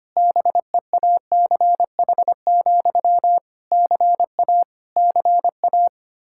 Morse code, heard as a keyed tone, about 25 wpm, 700 Hz. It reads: BEACH, CA CA
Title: Practice QSO - Exchange 1 - Beginning - 25wpm